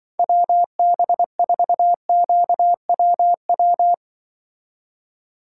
Morse code, heard as a keyed tone, about 25 wpm, 700 Hz. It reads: WB4QWW